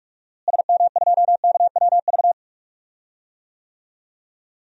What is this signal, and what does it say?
Morse code, heard as a keyed tone, about 47 wpm, 700 Hz. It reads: SM2KWV
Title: DXCC-100 Call Signs - 6 Characters - Repeated 2x - 45wpm